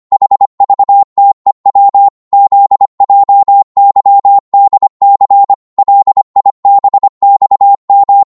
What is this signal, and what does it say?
Morse code, heard as a keyed tone, about 25 wpm, 800 Hz. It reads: HVTEW ZJYDC LIBXM
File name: Morse / ltr-25wpm